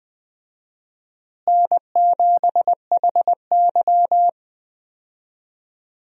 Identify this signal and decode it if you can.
Morse code, heard as a keyed tone, about 20 wpm, 700 Hz. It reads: N7HY
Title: USA Call Signs - 4 Characters - Repeated 2x - 20wpm